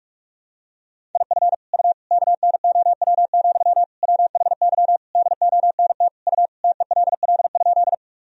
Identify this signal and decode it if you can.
Morse code, heard as a keyed tone, about 45 wpm, 700 Hz. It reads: IF U KNOW, WHY DONT U TELL?
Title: Sentences from Top 300 Words and Abbreviations 45wpm